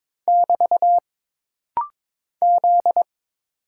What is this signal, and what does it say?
Morse code, heard as a keyed tone, about 23 wpm, 700 Hz. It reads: <BT> Z
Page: This rapid-fire timing is geared to push your ICR (Instant Character Recognition) skills.